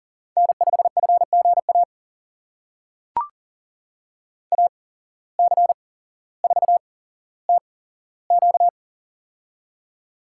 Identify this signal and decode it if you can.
Morse code, heard as a keyed tone, about 40 wpm, spaced as for 17 wpm, 700 Hz. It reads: N5FGA AC4TQ
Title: USA Call Signs - 5 Characters - 8x Character Spacing - 40wpm